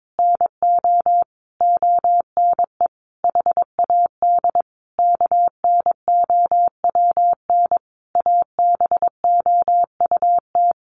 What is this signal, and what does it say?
Morse code, heard as a keyed tone, about 22 wpm, 700 Hz. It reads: NO ONE HAD KNOWN ABOUT
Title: Sentences from Top 1200 Words 22wpm